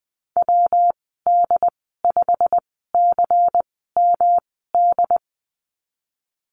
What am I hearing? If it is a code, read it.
Morse code, heard as a keyed tone, about 20 wpm, spaced as for 14 wpm, 700 Hz. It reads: WD5CMD
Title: 140 Lesson 23g - Letter f - Call Signs 20wpm